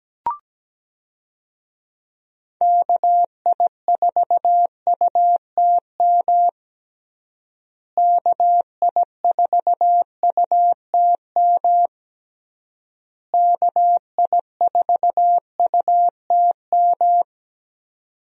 Morse code, heard as a keyed tone, about 17 wpm, 700 Hz. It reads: KI4UTM KI4UTM KI4UTM
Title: USA Call Signs - 6 Characters - Repeated 3x - 17wpm